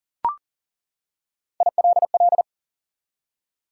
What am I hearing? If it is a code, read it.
Morse code, heard as a keyed tone, about 40 wpm, 700 Hz. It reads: ILL
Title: Three Letter Words - 40wpm